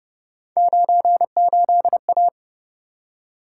Morse code, heard as a keyed tone, about 31 wpm, 700 Hz. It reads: 98A
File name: Speed-Racing - Three Letter-Number Combinations - 25wpm